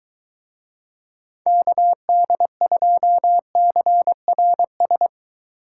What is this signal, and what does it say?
Morse code, heard as a keyed tone, about 23 wpm, 700 Hz. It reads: KD2CRS